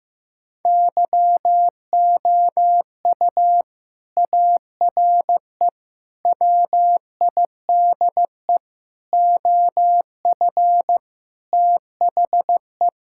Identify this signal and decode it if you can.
Morse code, heard as a keyed tone, about 16 wpm, 700 Hz. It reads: YOU ARE WIDE OF THE
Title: Sentences from Top 1000 Words - Review Part 6 - 15wpm